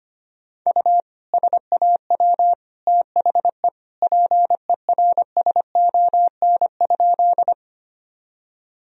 Morse code, heard as a keyed tone, about 25 wpm, 700 Hz. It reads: U SAW THE PERSON?
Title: Sentences from Top 400 Words and Abbreviations 25wpm